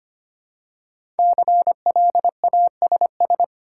Morse code, heard as a keyed tone, about 25 wpm, 700 Hz. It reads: CLASS
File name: 5 Letter Words - First 3 Letters Given - 25wpm